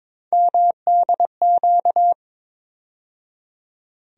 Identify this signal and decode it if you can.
Morse code, heard as a keyed tone, about 21 wpm, 700 Hz. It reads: MDQ
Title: USA Call Sign Postfixes - 3 Letters 22wpm